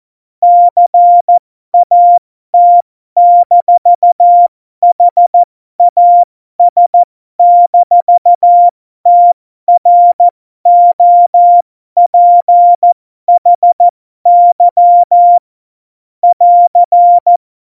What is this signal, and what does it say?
Morse code, heard as a keyed tone, about 14 wpm, spaced as for 12 wpm, 700 Hz. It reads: CAT-HAS-TROPHY <AR>
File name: Morsekurs / Witze